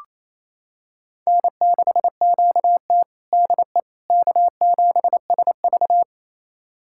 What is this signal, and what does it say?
Morse code, heard as a keyed tone, about 27 wpm, 700 Hz. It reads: N6QT DE K7SV
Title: USA Callsign DE Callsign - 4 Characters - 28wpm